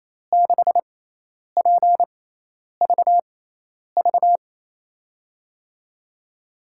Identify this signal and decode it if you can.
Morse code, heard as a keyed tone, about 28 wpm, spaced as for 10 wpm, 700 Hz. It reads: 6PVV
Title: USA Call Signs - 6 Characters - 6x Character Spacing - 28wpm